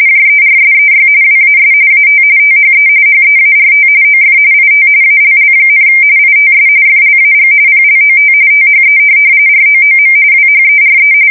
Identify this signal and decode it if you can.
RTTY, about 45.45 baud. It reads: JERSEY WAS AT THE HEART OF IT ALL, ITS FLAG PLOT THE ROOM WHERE
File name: fsk170 / USS-New-Jersey